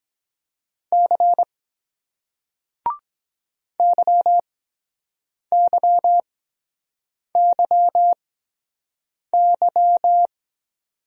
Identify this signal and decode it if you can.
Morse code, heard as a keyed tone, about 23 wpm, spaced as for 7 wpm, 700 Hz. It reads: C YYYY